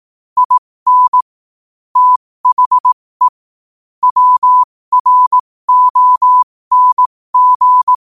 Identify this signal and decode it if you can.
Morse code, heard as a keyed tone, about 18 wpm, spaced as for 15 wpm, 1000 Hz. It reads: IN THE WRONG
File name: qotdfiles / 15wpm